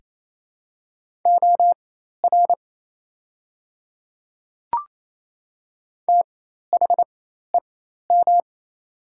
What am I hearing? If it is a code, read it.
Morse code, heard as a keyed tone, about 28 wpm, spaced as for 13 wpm, 700 Hz. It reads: OR THEM
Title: 4x Character Spacing - Top 100 Words - 28wpm